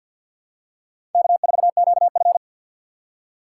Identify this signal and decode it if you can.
Morse code, heard as a keyed tone, about 53 wpm, 700 Hz. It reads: K4QF
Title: USA Call Signs - 3x Character Spacing - 50wpm